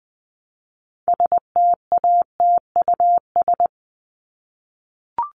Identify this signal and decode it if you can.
Morse code, heard as a keyed tone, about 20 wpm, 700 Hz. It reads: STATUS
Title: Top 1251-1500 Words 20wpm